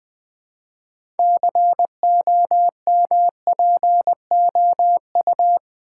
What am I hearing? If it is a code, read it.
Morse code, heard as a keyed tone, about 20 wpm, 700 Hz. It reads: COMPOU
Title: Build A Word - 2 Letters At A Time - 8-Letter Words - 20wpm